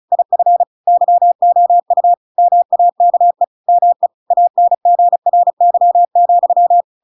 Morse code, heard as a keyed tone, about 34 wpm, 700 Hz. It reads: IF YOU MAKE ME ANGRY,